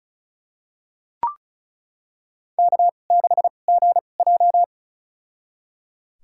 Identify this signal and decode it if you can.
Morse code, heard as a keyed tone, about 34 wpm, spaced as for 25 wpm, 700 Hz. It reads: K6GJ
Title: USA Call Signs - 4 Characters - 2x Character Spacing - 35wpm